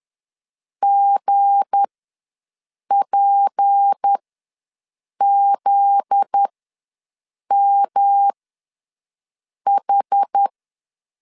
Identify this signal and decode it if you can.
Morse code, heard as a keyed tone, about 10 wpm, spaced as for 6 wpm, 800 Hz. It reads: GPZMH
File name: LEZIONI / 8 YJ45 / 8 YJ4 ws 10wpm 4020